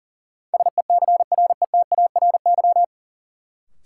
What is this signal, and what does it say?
Morse code, heard as a keyed tone, about 40 wpm, 700 Hz. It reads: SECRETARY
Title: Sentences from Top 1300 Words - New Word Drill Alt - 40wpm